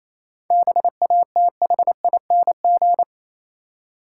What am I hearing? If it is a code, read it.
Morse code, heard as a keyed tone, about 28 wpm, 690 Hz. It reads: BATHING
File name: Easy Words - Speed-Racing with Hints - 22wpm